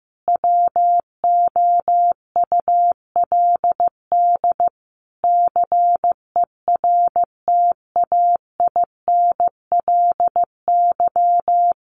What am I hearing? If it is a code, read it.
Morse code, heard as a keyed tone, about 15 wpm, 700 Hz. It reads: WOULD CERTAINLY
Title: Sentences from Top 900 Words 15wpm